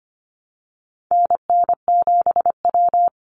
Morse code, heard as a keyed tone, about 25 wpm, 700 Hz. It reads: NN7W
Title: Call Signs - 25wpm